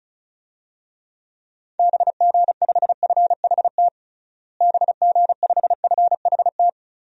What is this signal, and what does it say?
Morse code, heard as a keyed tone, about 34 wpm, 700 Hz. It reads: BG5FHT BG5FHT
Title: DXCC-100 Call Signs - 6 Characters - Repeated 3x - 35wpm